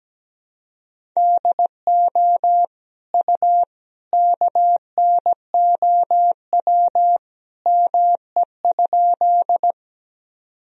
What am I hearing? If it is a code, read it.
Morse code, heard as a keyed tone, about 17 wpm, 700 Hz. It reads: DO U KNOW ME?
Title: Sentences from Top 200 Words and Abbreviations 17wpm